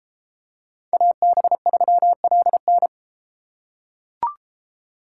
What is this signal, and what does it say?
Morse code, heard as a keyed tone, about 32 wpm, 700 Hz. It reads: AB3LN